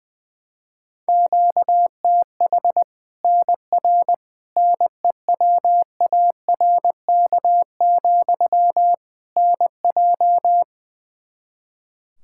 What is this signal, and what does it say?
Morse code, heard as a keyed tone, about 20 wpm, 700 Hz. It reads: QTH NR NEWARK, NJ
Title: Practice QSO - Exchange 1 - Beginning - 20wpm